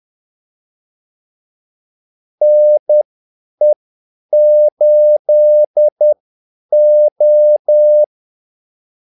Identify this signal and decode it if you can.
Morse code, heard as a keyed tone, about 10 wpm, spaced as for 8 wpm, 600 Hz. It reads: NE8O
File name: Callsigns-10-8wpm